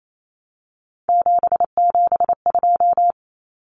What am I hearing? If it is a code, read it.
Morse code, heard as a keyed tone, about 28 wpm, 700 Hz. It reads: 772
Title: Numbers - 3-Digit 28wpm